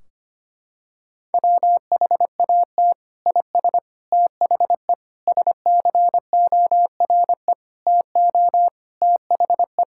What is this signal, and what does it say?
Morse code, heard as a keyed tone, about 25 wpm, 700 Hz. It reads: WHAT IS THE SCORE TO THE
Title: Sentences from Top 1600 Words 25wpm